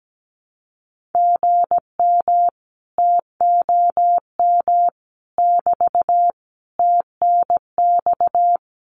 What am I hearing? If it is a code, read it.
Morse code, heard as a keyed tone, about 17 wpm, 700 Hz. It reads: GM TOM <BT> TNX
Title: Common QSO Phrases Encore 17wpm